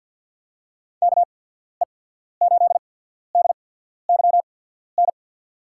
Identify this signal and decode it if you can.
Morse code, heard as a keyed tone, about 53 wpm, spaced as for 14 wpm, 700 Hz. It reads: KE8DYN
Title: USA Call Signs - 6 Characters - 8x Character Spacing - 50wpm